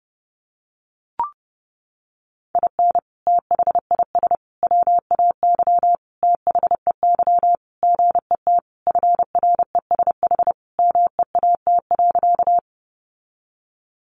Morse code, heard as a keyed tone, about 30 wpm, 700 Hz. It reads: IN THIS WAY THEY GET FRESH MEAT.
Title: Sentences from Top 1200 Words 30wpm